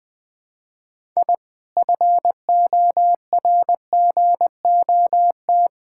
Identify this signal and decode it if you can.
Morse code, heard as a keyed tone, about 21 wpm, 700 Hz. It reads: I FORGOT
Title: Sentences with 4-Word Phrases - 20wpm